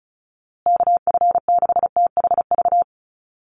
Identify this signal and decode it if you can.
Morse code, heard as a keyed tone, about 34 wpm, 700 Hz. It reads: KF6THV
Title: USA Call Signs Repeated 2x 35wpm